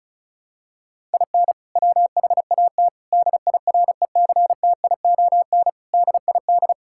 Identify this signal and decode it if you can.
Morse code, heard as a keyed tone, about 36 wpm, 700 Hz. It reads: IN WHAT DIRECTION DID
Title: Sentences from Top 1000 Words - Review Part 5 - 35wpm